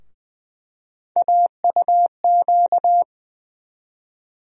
Morse code, heard as a keyed tone, about 20 wpm, 700 Hz. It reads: AUQ
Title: USA Call Sign Postfixes - 3 Letters 20wpm